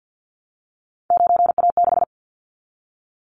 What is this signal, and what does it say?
Morse code, heard as a keyed tone, about 50 wpm, 700 Hz. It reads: 9A6
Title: Callsign Prefixes for Top 50 DXCC - 50wpm